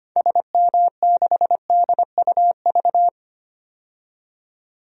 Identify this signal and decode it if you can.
Morse code, heard as a keyed tone, about 25 wpm, 700 Hz. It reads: SM6DUV
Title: DXCC-100 Call Signs - 6 Characters - 25wpm